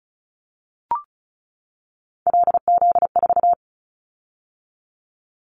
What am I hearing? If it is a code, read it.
Morse code, heard as a keyed tone, about 34 wpm, 700 Hz. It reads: LZ4
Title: Callsign Prefixes for Top 100 DXCC - 35wpm